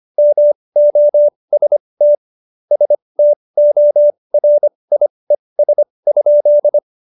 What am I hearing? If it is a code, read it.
Morse code, heard as a keyed tone, about 25 wpm, spaced as for 20 wpm, 590 Hz. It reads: MOST STORIES?
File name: Jokes-25-char-20-eff